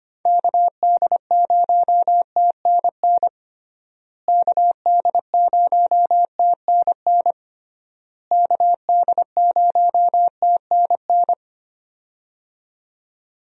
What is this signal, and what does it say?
Morse code, heard as a keyed tone, about 25 wpm, 700 Hz. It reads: KD0TNN KD0TNN KD0TNN
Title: USA Call Signs - 6 Characters - Repeated 3x - 25wpm